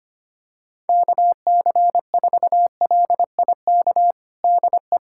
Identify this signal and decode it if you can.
Morse code, heard as a keyed tone, about 25 wpm, 700 Hz. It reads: KC4LIK DE
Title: USA Callsign DE Callsign - 6 Characters - 25wpm